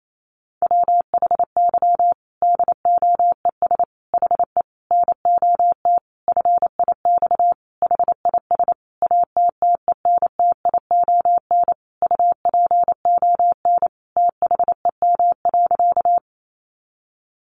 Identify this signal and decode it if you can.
Morse code, heard as a keyed tone, about 28 wpm, 700 Hz. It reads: WHY DOES HE NOT FIX HIS ATTENTION UPON THEM.
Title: Sentences from Top 1400 Words 28wpm